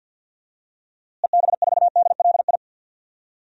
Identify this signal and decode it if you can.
Morse code, heard as a keyed tone, about 47 wpm, 700 Hz. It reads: EB4DLI
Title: DXCC-100 Call Signs - 6 Characters - Repeated 2x - 50wpm